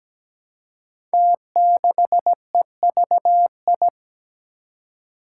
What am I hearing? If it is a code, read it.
Morse code, heard as a keyed tone, about 17 wpm, 700 Hz. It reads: T6EVI
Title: Five Challenging Letter-Number Combinations 17wpm